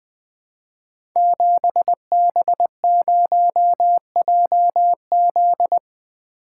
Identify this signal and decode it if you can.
Morse code, heard as a keyed tone, about 20 wpm, 700 Hz. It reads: 7B0JZ
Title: Code Groups - 5 Letter:Numbers - 20wpm